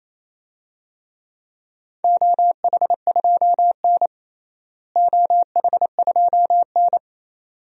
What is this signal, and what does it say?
Morse code, heard as a keyed tone, about 28 wpm, 700 Hz. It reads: OH2N OH2N
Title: DXCC-100 Call Signs - 4 Characters - Repeated 2x - 28wpm